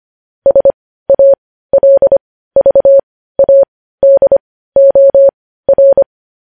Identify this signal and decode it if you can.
Morse code, heard as a keyed tone, about 25 wpm, spaced as for 15 wpm, 550 Hz. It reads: SALVADOR
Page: news-headlines-in-morse-code-at-15-wpm